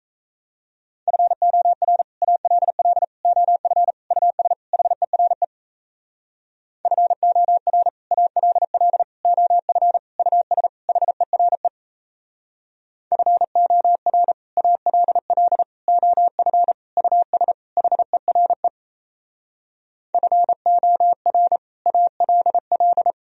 Morse code, heard as a keyed tone, about 36 wpm, 700 Hz. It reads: FOR ALL OF US HERE FOR ALL OF US HERE FOR ALL OF US HERE FOR ALL